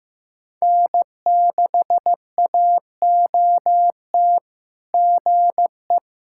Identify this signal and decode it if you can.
Morse code, heard as a keyed tone, about 15 wpm, 690 Hz. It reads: N6AOT GE
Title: SOTA QSO Practice 15wpm